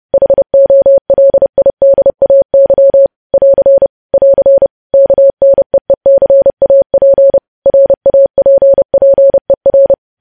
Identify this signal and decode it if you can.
Morse code, heard as a keyed tone, about 30 wpm, 550 Hz. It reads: HOLIDAY <AR> <AR> KNEECAP RAPPER
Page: news headlines in morse code at 30 wpm